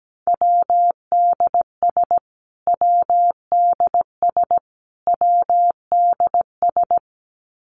Morse code, heard as a keyed tone, about 17 wpm, 700 Hz. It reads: WDS WDS WDS
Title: Abbreviations Repeated 3x - 17wpm